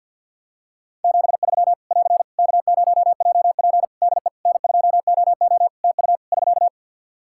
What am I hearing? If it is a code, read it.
Morse code, heard as a keyed tone, about 51 wpm, 700 Hz. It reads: 73 <AR> K0JP DE N2OO TU <SK>
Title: Practice QSO - Exchange 4 - Ending - 50wpm